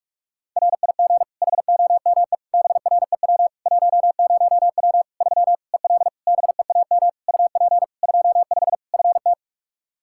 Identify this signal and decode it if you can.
Morse code, heard as a keyed tone, about 46 wpm, 700 Hz. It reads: RIG HOME BREW 10W 3 EL BEAM UP 25 FT
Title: K1USN SST QSO Practice 45wpm